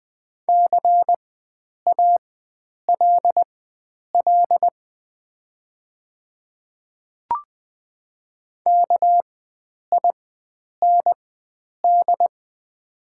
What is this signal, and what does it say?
Morse code, heard as a keyed tone, about 20 wpm, spaced as for 9 wpm, 700 Hz. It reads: CALL KIND
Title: 4x Character Spacing - Top 500 Words - 20wpm